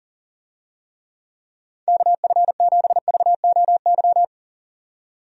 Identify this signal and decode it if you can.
Morse code, heard as a keyed tone, about 39 wpm, 700 Hz. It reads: KF7VOY
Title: USA Call Signs - 6 Characters - 40wpm